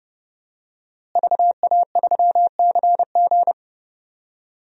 Morse code, heard as a keyed tone, about 30 wpm, 700 Hz. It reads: VA3CG
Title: DXCC-100 Call Signs - 5 Characters - Repeated 2x - 30wpm